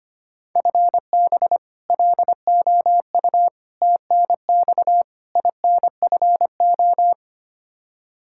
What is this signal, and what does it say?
Morse code, heard as a keyed tone, about 25 wpm, 700 Hz. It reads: FB LOU TNX INFO
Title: K1USN SST QSO Practice 25wpm